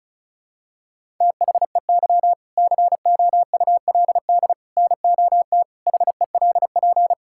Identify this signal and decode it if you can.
Morse code, heard as a keyed tone, about 36 wpm, 700 Hz. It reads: THEY COULD NOT HELP
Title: Sentences from Top 1000 Words - Review Part 5 - 35wpm